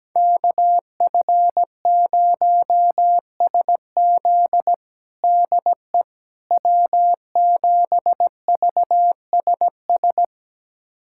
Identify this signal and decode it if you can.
Morse code, heard as a keyed tone, about 17 wpm, 700 Hz. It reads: KF0SZ DE W7VSS
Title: USA Callsign DE Callsign - 5 Characters - 17wpm